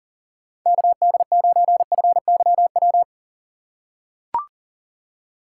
Morse code, heard as a keyed tone, about 40 wpm, 700 Hz. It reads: KD9FYW
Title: USA Call Signs - 6 Characters - 8x Character Spacing - 40wpm